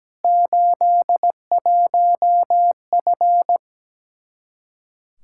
Morse code, heard as a keyed tone, about 17 wpm, 700 Hz. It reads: 81F
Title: Practice QSO - Exchange 2 - Info - 17wpm